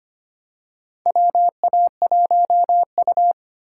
Morse code, heard as a keyed tone, about 25 wpm, 700 Hz. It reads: WA1U
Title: USA Call Signs - 4 Characters - 2x Character Spacing - 25wpm